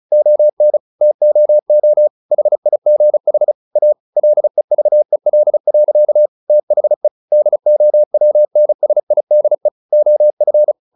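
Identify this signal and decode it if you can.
Morse code, heard as a keyed tone, about 34 wpm, 600 Hz. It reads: ON TOO HIGH A LEVEL. THE DOWNSIDE OF
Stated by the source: Morse practice at 35 wpm from CW med Gnister, sending Ken Follett's novel Never in English.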